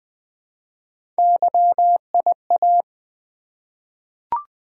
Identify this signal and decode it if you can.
Morse code, heard as a keyed tone, about 20 wpm, 700 Hz. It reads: YIA
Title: Three Letter-Number Combinations - 20wpm